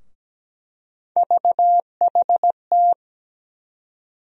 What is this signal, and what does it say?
Morse code, heard as a keyed tone, about 17 wpm, 700 Hz. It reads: VHT
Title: USA Call Sign Postfixes - 3 Letters 17wpm